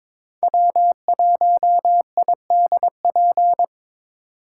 Morse code, heard as a keyed tone, about 22 wpm, 700 Hz. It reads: W1IDP
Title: USA Call Signs - 6x Character Spacing - 22wpm